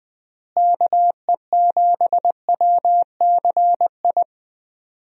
Morse code, heard as a keyed tone, about 20 wpm, 700 Hz. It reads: KE7WCI
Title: USA Call Signs - 6 Characters - Repeated 3x - 20wpm